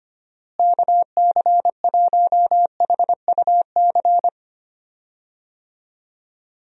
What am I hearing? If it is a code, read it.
Morse code, heard as a keyed tone, about 25 wpm, 700 Hz. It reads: KC1HUC
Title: USA Call Signs - 6 Characters - Repeated 2x - 25wpm